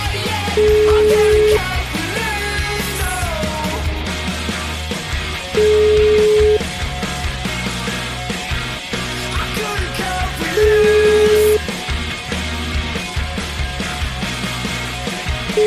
Ringback tone (Europe, 425 Hz), with noise.